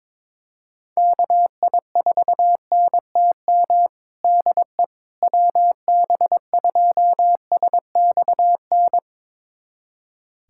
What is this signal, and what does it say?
Morse code, heard as a keyed tone, about 22 wpm, 700 Hz. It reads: KI4NTM DE WB2SXN
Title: USA Callsign DE Callsign 22wpm